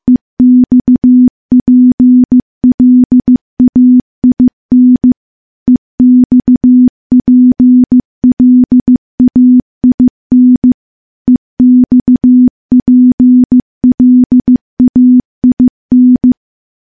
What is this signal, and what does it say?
Morse code, heard as a keyed tone, about 15 wpm, 250 Hz. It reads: EXPLAIN EXPLAIN EXPLAIN
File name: morse / morse-training-15WPM